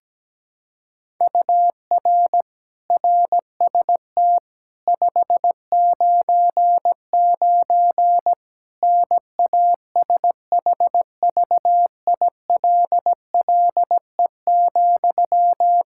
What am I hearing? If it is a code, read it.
Morse code, heard as a keyed tone, about 17 wpm, 700 Hz. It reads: UR RST 599 NASHVILLE,
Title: Practice QSO - Exchange 1 - Beginning - 17wpm